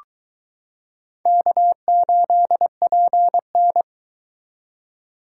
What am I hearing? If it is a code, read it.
Morse code, heard as a keyed tone, about 22 wpm, 700 Hz. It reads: K8PN